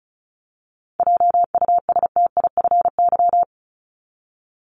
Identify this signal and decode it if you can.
Morse code, heard as a keyed tone, about 34 wpm, 700 Hz. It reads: JUSTIFY
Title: Top 1751-2000 Words 35wpm